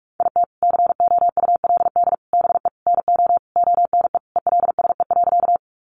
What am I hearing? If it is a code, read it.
Morse code, heard as a keyed tone, about 44 wpm, 700 Hz. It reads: IT COULD BE NO ONE ELSE.
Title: Sentences from Top 800 Words 45wpm